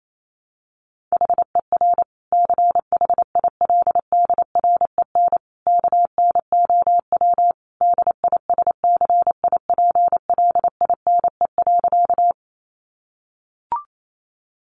Morse code, heard as a keyed tone, about 28 wpm, 690 Hz. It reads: HER CHILDREN KNOW DISCIPLINE.
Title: Sentences from Top 900 Words 28wpm